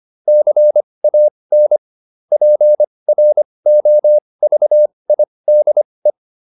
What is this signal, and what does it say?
Morse code, heard as a keyed tone, about 25 wpm, spaced as for 20 wpm, 600 Hz. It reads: CAN PROVIDE